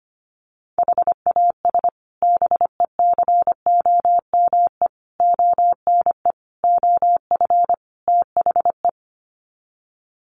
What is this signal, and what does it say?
Morse code, heard as a keyed tone, about 25 wpm, 700 Hz. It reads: HAS BECOME ONE OF THE
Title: Sets of 5 Words Encore 25wpm